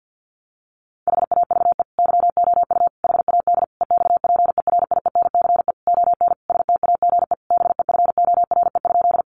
Morse code, heard as a keyed tone, about 50 wpm, 700 Hz. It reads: HAVE YOU HAD EXPERIENCE ON STAGE BEFORE?
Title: Sentences from Top 1200 Words 50wpm